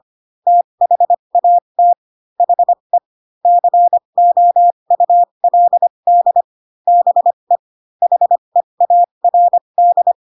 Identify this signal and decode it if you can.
Morse code, heard as a keyed tone, about 25 wpm, spaced as for 22 wpm, 700 Hz. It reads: THAT HE COULD BE HEARD
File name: Pinocchio / Pinocchio ch7 18 25 ik1hge